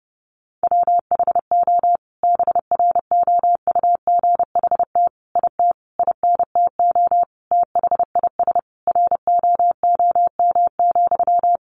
Morse code, heard as a keyed tone, about 30 wpm, 700 Hz. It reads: WHO BROUGHT IT INTO THIS ROOM,
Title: Sentences from Top 600 Words 30wpm